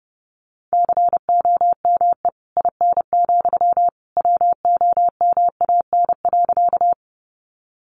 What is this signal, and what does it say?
Morse code, heard as a keyed tone, about 30 wpm, 700 Hz. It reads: COME IN, WOMAN.
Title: Sentences from Top 1100 Words 30wpm